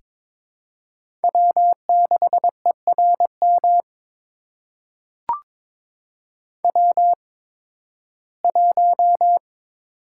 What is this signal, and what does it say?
Morse code, heard as a keyed tone, about 22 wpm, spaced as for 14 wpm, 700 Hz. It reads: W6ERM W1
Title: USA Call Signs - 5 Characters - 8x Character Spacing - 22wpm